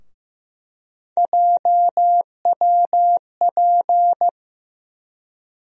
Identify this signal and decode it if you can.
Morse code, heard as a keyed tone, about 15 wpm, 700 Hz. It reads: JWP
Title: USA Call Sign Postfixes - 3 Letters 15wpm